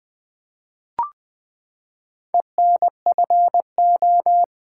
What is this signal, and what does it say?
Morse code, heard as a keyed tone, about 20 wpm, 700 Hz. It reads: ENFO
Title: Build A Word - 4 Letters At A Time - 11-Letter Words - 20wpm